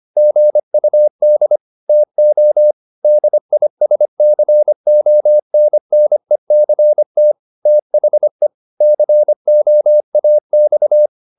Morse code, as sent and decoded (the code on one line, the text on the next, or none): --. ..- -.. / - --- / -.. .. ... -.-. --- -. -. -.-. - / - .... / -.-. --- .- -..-
GUD TO DISCONNECT THE COAX